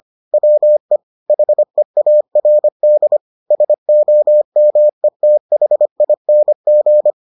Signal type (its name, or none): Morse code